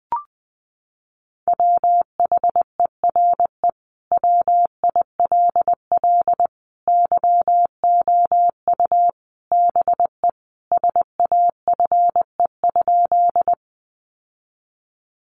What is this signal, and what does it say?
Morse code, heard as a keyed tone, about 19 wpm, 700 Hz. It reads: WHERE WILL YOU BE SAFE?
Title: Sentences from Top 900 Words 20wpm